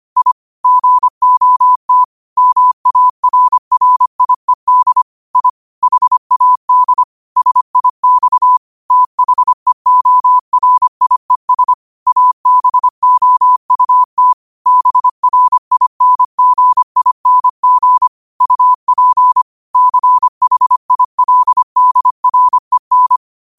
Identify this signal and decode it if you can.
Morse code, heard as a keyed tone, about 25 wpm, 1000 Hz. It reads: I GOT MARRIED I HAD SIX THEORIES ABOUT BRINGING UP CHILDREN